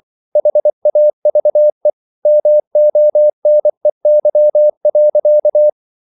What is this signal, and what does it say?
Morse code, heard as a keyed tone, about 24 wpm, 600 Hz. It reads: HAVE MONEY.